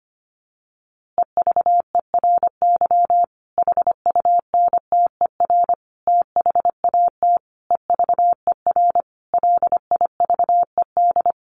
Morse code, heard as a keyed tone, about 25 wpm, 700 Hz. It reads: EVERY HUNTER THAT EVER LIVED
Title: A Sound Of Thunder 25wpm